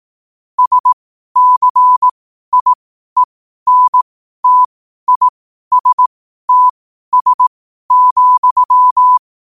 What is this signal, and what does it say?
Morse code, heard as a keyed tone, about 18 wpm, spaced as for 12 wpm, 1000 Hz. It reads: SCIENTISTS,
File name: qotdfiles / 12wpm